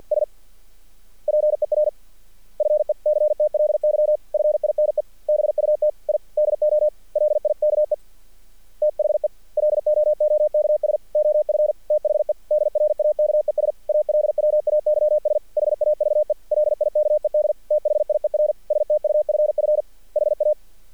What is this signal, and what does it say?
Morse code, heard as a keyed tone, about 48 wpm, 600 Hz. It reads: S PERFECTLY FINE BUT I DO LIKE THE LOOKS OF THE DRAKES ALWAYS HAVE LIKED THIER STUFF HA
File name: archived / CRSnet